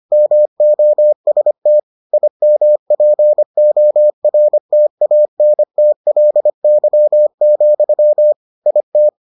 Morse code, heard as a keyed tone, about 25 wpm, 600 Hz. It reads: MOST IMPORTANTLY, IT